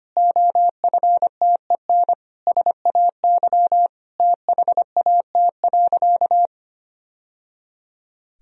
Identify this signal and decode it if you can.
Morse code, heard as a keyed tone, about 25 wpm, 700 Hz. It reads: OFTEN SAY THAT.
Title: Sentences from Top 1000 Words - Review Part 1 - 25wpm